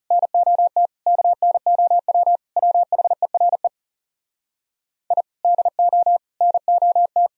Morse code, heard as a keyed tone, about 39 wpm, 700 Hz. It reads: NOT KNOW WHERE I DO NOT